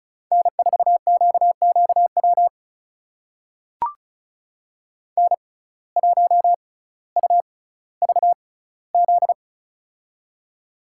Morse code, heard as a keyed tone, about 36 wpm, spaced as for 18 wpm, 700 Hz. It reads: N4QQW N1UVZ
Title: USA Call Signs - 5 Characters - 6x Character Spacing - 35wpm